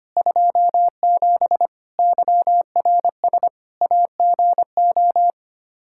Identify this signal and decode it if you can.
Morse code, heard as a keyed tone, about 25 wpm, 700 Hz. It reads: 27 YRS AGO
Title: Practice QSO - Exchange 3 - More Info - 25wpm